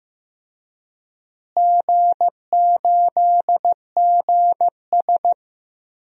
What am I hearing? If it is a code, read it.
Morse code, heard as a keyed tone, about 15 wpm, 700 Hz. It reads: G8GS
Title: DXCC-100 Call Signs - 4 Characters - Repeated 2x - 15wpm